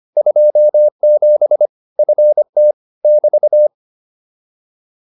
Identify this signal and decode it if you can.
Morse code, heard as a keyed tone, about 25 wpm, 600 Hz. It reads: 27 FT <BT>
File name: QSO307a 25